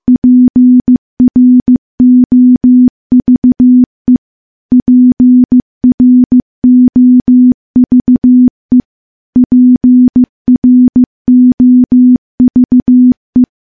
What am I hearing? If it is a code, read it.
Morse code, heard as a keyed tone, about 15 wpm, 250 Hz. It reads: PROVE PROVE PROVE
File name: morse / morse-training-15WPM